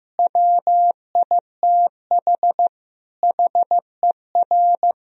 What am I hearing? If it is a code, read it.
Morse code, heard as a keyed tone, about 15 wpm, 700 Hz. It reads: WITH HER
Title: Sentences from Top 1000 Words - Review Part 3 - 15wpm